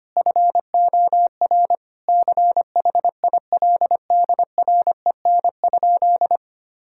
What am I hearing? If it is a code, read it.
Morse code, heard as a keyed tone, about 25 wpm, 700 Hz. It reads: FOR CHILDREN?
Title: Sentences from Top 1000 Words - Review Part 5 - 25wpm